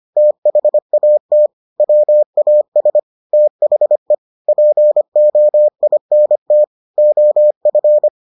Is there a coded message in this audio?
Yes: Morse code from 0.2 to 8.1 s, heard as a keyed tone, about 25 wpm, 600 Hz. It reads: THAT WAS THE POINT OF